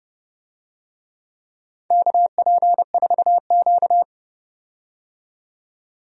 Morse code, heard as a keyed tone, about 31 wpm, 700 Hz. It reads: KP4Q
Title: DXCC-100 Call Signs - 4 Characters - 30wpm